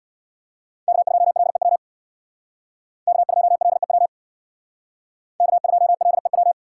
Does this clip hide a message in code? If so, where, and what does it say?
Yes: Morse code from 0.9 to 6.5 s, heard as a keyed tone, about 63 wpm, 700 Hz. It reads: K2LEW K2LEW K2LEW